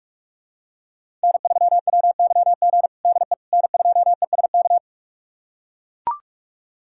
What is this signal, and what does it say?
Morse code, heard as a keyed tone, about 41 wpm, 700 Hz. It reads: N3WYG DE N2ESK
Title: USA Callsign DE Callsign - 5 Characters - 45wpm